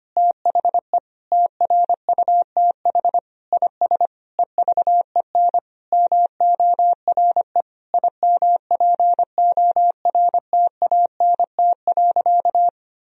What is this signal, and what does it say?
Morse code, heard as a keyed tone, about 25 wpm, 700 Hz. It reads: THE TRUTH IS EVEN MORE IMPORTANT.
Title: Sentences from Top 1000 Words - Review Part 3 - 25wpm